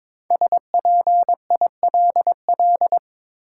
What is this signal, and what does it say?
Morse code, heard as a keyed tone, about 22 wpm, 700 Hz. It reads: SPILL
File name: 5 Letter Words - First 3 Letters Given - 22wpm